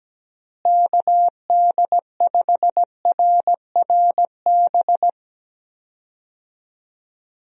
Morse code, heard as a keyed tone, about 17 wpm, 700 Hz. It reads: KD5RRB
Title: USA Call Signs - 6 Characters - Repeated 2x - 17wpm